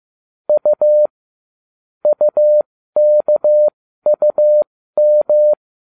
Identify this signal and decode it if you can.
Morse code, heard as a keyed tone, about 15 wpm, spaced as for 13 wpm, 600 Hz. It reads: U UKUM